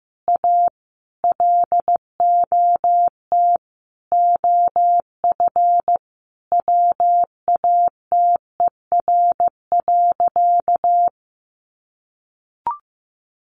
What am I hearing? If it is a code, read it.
Morse code, heard as a keyed tone, about 15 wpm, 700 Hz. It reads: A LOT OF WATER.
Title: Additional Easy Phrases 15wpm